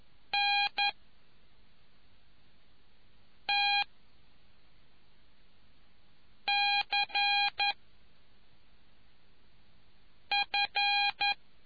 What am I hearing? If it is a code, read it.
Morse code, heard as a keyed tone, about 11 wpm, 2450 Hz. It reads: N T C F